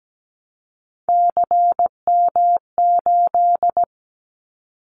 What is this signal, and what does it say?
Morse code, heard as a keyed tone, about 17 wpm, 700 Hz. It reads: CM8
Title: Callsign Prefixes for Top 100 DXCC - 17wpm